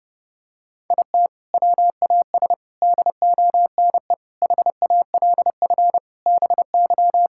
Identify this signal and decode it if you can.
Morse code, heard as a keyed tone, about 30 wpm, 700 Hz. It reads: IT WAS DONE HALF BY
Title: Sentences from Top 1000 Words - Review Part 5 - 30wpm